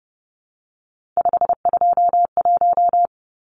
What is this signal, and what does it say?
Morse code, heard as a keyed tone, about 30 wpm, 700 Hz. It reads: 521
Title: Numbers - 3-Digit 30wpm